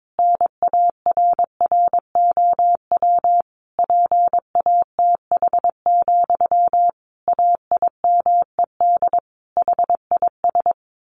Morse code, heard as a keyed tone, about 22 wpm, 700 Hz. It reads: NARROW PATH, AIMED HIS
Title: A Sound Of Thunder 22wpm